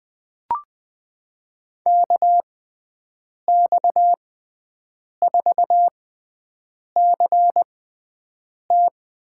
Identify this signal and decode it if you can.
Morse code, heard as a keyed tone, about 20 wpm, spaced as for 7 wpm, 700 Hz. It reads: KX4CT
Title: USA Call Signs - 5 Characters - 6x Character Spacing - 20wpm